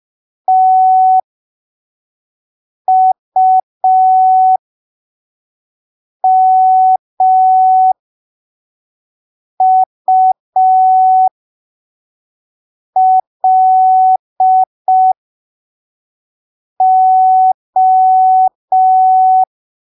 Morse code, heard as a keyed tone, about 5 wpm, spaced as for 3 wpm, 750 Hz. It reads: TUMULO